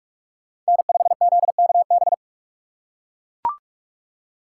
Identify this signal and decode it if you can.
Morse code, heard as a keyed tone, about 43 wpm, 700 Hz. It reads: N5ZKB
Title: USA Call Signs - 5 Characters - 4x Character Spacing - 45wpm